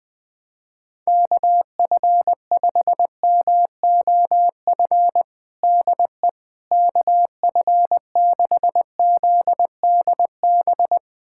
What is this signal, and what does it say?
Morse code, heard as a keyed tone, about 20 wpm, 700 Hz. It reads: KF5MOF DE KF6ZDB
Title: USA Callsign DE Callsign - 6 Characters - 20wpm